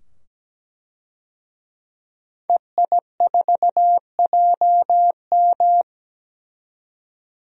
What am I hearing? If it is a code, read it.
Morse code, heard as a keyed tone, about 17 wpm, 700 Hz. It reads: EI4JM
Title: DXCC-100 Call Signs - 5 Characters - 17wpm